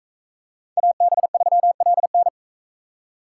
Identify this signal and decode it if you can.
Morse code, heard as a keyed tone, about 40 wpm, 700 Hz. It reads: AB3LN